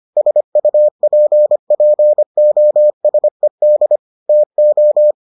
Morse code, heard as a keyed tone, about 25 wpm, 600 Hz. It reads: SUPPOSED TO